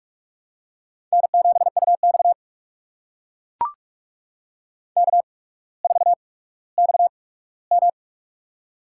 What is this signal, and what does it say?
Morse code, heard as a keyed tone, about 44 wpm, spaced as for 19 wpm, 700 Hz. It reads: N7UX K4XM
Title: USA Call Signs - 4 Characters - 8x Character Spacing - 45wpm